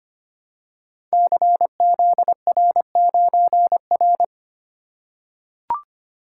Morse code, heard as a keyed tone, about 25 wpm, 700 Hz. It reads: CZR9R
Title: Code Groups - 5 Letter:Numbers - 25wpm